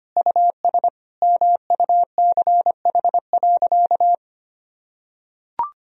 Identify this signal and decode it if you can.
Morse code, heard as a keyed tone, about 25 wpm, 700 Hz. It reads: US MUCH.
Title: Sentences from Top 1000 Words - Review Part 2 - 25wpm